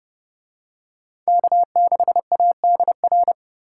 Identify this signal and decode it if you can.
Morse code, heard as a keyed tone, about 29 wpm, 700 Hz. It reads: K6ADR